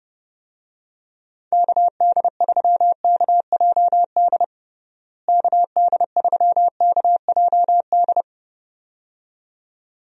Morse code, heard as a keyed tone, about 29 wpm, 700 Hz. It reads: KD3KJD KD3KJD
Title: USA Call Signs - 6 Characters - Repeated 2x - 30wpm